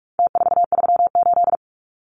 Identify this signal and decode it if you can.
Morse code, heard as a keyed tone, about 44 wpm, 700 Hz. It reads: T438
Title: Numbers - Contest Style - 4-Digit 45wpm